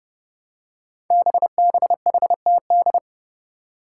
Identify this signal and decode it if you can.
Morse code, heard as a keyed tone, about 30 wpm, 700 Hz. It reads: BBHTD
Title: Five Challenging Letter-Number Combinations 30wpm